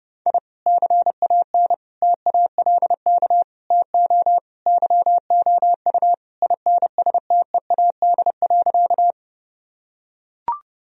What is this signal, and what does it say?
Morse code, heard as a keyed tone, about 32 wpm, 700 Hz. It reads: I CAN TALK TO YOU INSTEAD.
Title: Sentences from Top 1000 Words - Review Part 4 - 30wpm